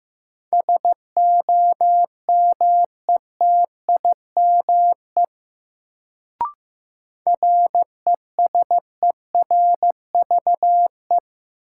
Morse code, heard as a keyed tone, about 15 wpm, 700 Hz. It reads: SOMETIME RESERVE
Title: Sentences from Top 1000 Words - Review Part 6 - New Word Drill - 15wpm